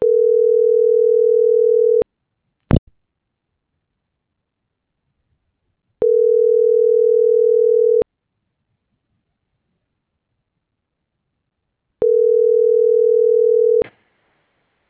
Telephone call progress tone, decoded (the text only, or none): ringback tone